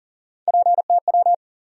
Morse code, heard as a keyed tone, about 40 wpm, 700 Hz. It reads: PTW
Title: USA Call Sign Postfixes - 3 Letters 40wpm